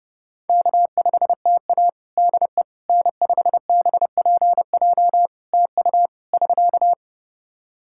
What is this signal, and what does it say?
Morse code, heard as a keyed tone, about 32 wpm, 700 Hz. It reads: K5TA DE N5BPJ TU <SK>
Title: K1USN SST QSO Practice 30wpm